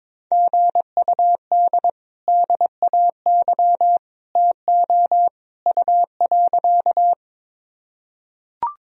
Morse code, heard as a keyed tone, about 23 wpm, 700 Hz. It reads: GUD DAY TO U.
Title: Sentences from Top 200 Words and Abbreviations 22wpm